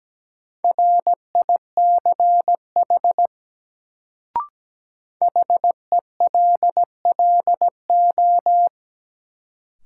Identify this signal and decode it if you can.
Morse code, heard as a keyed tone, about 17 wpm, 700 Hz. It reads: RICH HELLO
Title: Sentences from Top 1100 Words - New Word Drill - 17wpm